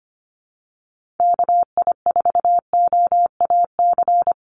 Morse code, heard as a keyed tone, about 25 wpm, 700 Hz. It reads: KI4OAC
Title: USA Call Signs 25wpm